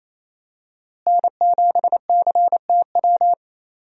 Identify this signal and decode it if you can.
Morse code, heard as a keyed tone, about 28 wpm, 700 Hz. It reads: N7CTW
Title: USA Call Signs - 5 Characters - Repeated 3x - 28wpm